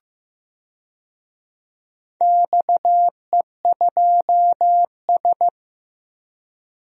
Morse code, heard as a keyed tone, about 15 wpm, 700 Hz. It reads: XE2S
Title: DXCC-100 Call Signs - 4 Characters - Repeated 3x - 15wpm